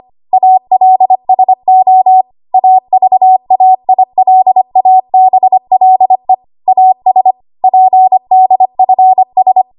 Morse code, heard as a keyed tone, about 25 wpm, 750 Hz. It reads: ALSO AVAILABLE AS PDFS